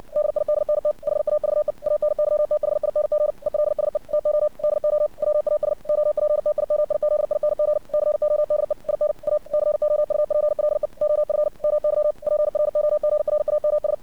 Morse code, heard as a keyed tone, about 53 wpm, 600 Hz. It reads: DIDNT HAVE ANYTHING ELSE TO DO WAS OPTIMIZING CODE IN A COUPLE OF MY PROGRAMS
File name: archived / CRSnet